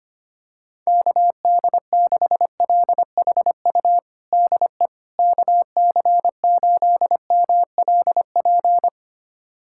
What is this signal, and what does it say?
Morse code, heard as a keyed tone, about 25 wpm, 700 Hz. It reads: KD6LHU DE KC8MLP
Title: USA Callsign DE Callsign 25wpm